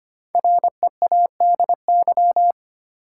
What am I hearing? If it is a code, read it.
Morse code, heard as a keyed tone, about 25 wpm, 700 Hz. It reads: READY
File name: 5 Letter Words - First 3 Letters Given - 25wpm